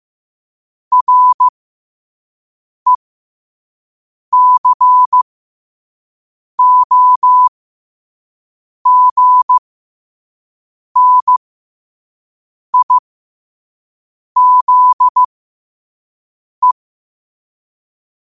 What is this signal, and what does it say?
Morse code, heard as a keyed tone, about 15 wpm, spaced as for 5 wpm, 1000 Hz. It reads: RECOGNIZE
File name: qotdfiles / 5wpm